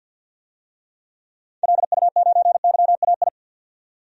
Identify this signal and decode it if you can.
Morse code, heard as a keyed tone, about 52 wpm, 700 Hz. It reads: LU9YAI
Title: DXCC-100 Call Signs - 6 Characters - 50wpm